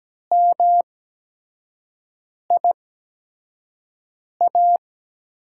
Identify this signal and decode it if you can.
Morse code, heard as a keyed tone, about 17 wpm, spaced as for 5 wpm, 700 Hz. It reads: MIA
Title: USA Call Signs - 5 Characters - 8x Character Spacing - 17wpm